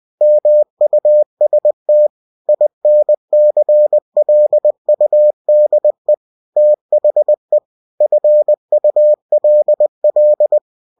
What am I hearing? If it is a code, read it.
Morse code, heard as a keyed tone, about 20 wpm, 600 Hz. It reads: MUST INCLUDE THE FULL
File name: cw-books / winnie-the-pooh / winnie-20wpm-page-0116